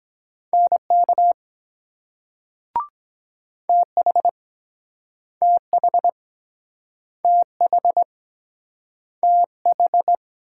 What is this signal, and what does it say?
Morse code, heard as a keyed tone, about 23 wpm, 700 Hz. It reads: NK TH TH TH TH